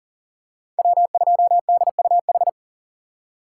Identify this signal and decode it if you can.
Morse code, heard as a keyed tone, about 37 wpm, 700 Hz. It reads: W2DUH
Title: USA Call Signs - 5 Characters - 3x Character Spacing - 40wpm